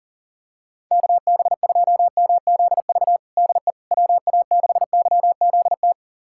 Morse code, heard as a keyed tone, about 40 wpm, 700 Hz. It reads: KB2MZV DE WA6YZT
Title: USA Callsign DE Callsign - 6 Characters - 40wpm